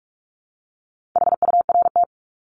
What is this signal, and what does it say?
Morse code, heard as a keyed tone, about 44 wpm, 700 Hz. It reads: HURT
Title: Top 1751-2000 Words 45wpm